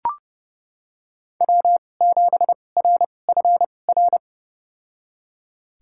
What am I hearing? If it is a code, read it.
Morse code, heard as a keyed tone, about 30 wpm, spaced as for 22 wpm, 700 Hz. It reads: W7RFR
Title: USA Call Signs - 5 Characters - 2x Character Spacing - 30wpm